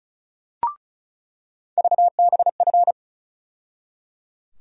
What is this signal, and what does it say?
Morse code, heard as a keyed tone, about 36 wpm, 700 Hz. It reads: VBF
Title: USA Call Sign Postfixes - 3 Letters 35wpm